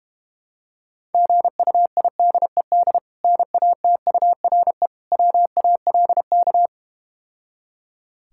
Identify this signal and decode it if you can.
Morse code, heard as a keyed tone, about 31 wpm, 700 Hz. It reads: GUIDED NATURE WALK
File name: practice / 25wpm / Difficult Words - Speed-Racing with Hints - 25wpm